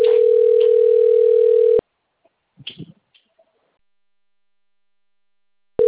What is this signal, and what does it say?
Ringback tone (North America), 440 + 480 Hz.